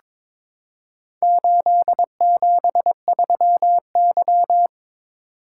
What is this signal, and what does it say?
Morse code, heard as a keyed tone, about 21 wpm, 700 Hz. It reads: 873Y
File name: practice / 22wpm / Four Letter-Number Combinations - 22wpm